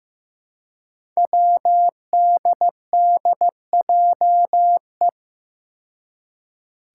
Morse code, heard as a keyed tone, about 15 wpm, 700 Hz. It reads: WDDJE
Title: Five Challenging Letter-Number Combinations 15wpm